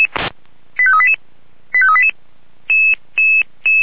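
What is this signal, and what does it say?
Five-tone selective call (ZVEI), heard as quick runs of five tones, tones of about 70 ms. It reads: 85388 85388
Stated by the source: ZVEI five-tone call received on an SDR.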